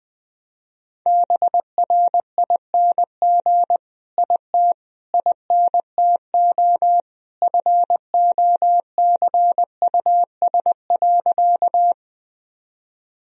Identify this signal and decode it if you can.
Morse code, heard as a keyed tone, about 20 wpm, 700 Hz. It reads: BRING IT INTO FOCUS.
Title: Sentences from Top 1000 Words - Review Part 4 - 20wpm